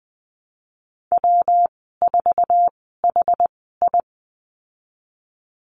Morse code, heard as a keyed tone, about 20 wpm, spaced as for 15 wpm, 700 Hz. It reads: W4HI
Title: 122 Lesson 20g - Number 3 - Call Signs 20wpm